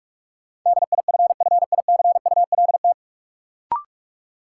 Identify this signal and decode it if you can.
Morse code, heard as a keyed tone, about 46 wpm, 700 Hz. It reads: DIFFICULT
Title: Sets of 4 Words - New Word Drill - 45wpm